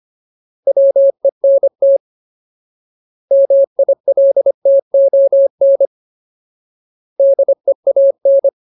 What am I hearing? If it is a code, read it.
Morse code, heard as a keyed tone, about 25 wpm, 550 Hz. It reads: WENT MILTON DEAN
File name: Ultimate Word List / 25WPM Ultimate